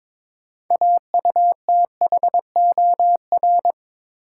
Morse code, heard as a keyed tone, about 22 wpm, 700 Hz. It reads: AUTHOR
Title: Sentences from Top 1200 Words - New Word Drill Alt - 22wpm